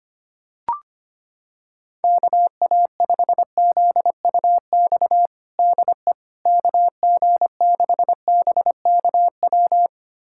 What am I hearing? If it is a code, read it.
Morse code, heard as a keyed tone, about 25 wpm, 700 Hz. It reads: KA5ZUX DE KG6BKW
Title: USA Callsign DE Callsign 25wpm